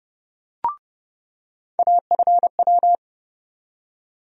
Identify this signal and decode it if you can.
Morse code, heard as a keyed tone, about 30 wpm, 700 Hz. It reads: AFW
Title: USA Call Sign Postfixes - 3 Letters 30wpm